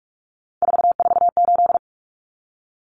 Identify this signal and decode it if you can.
Morse code, heard as a keyed tone, about 44 wpm, 700 Hz. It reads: 448
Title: Common Cut Numbers - 3-Digit 45wpm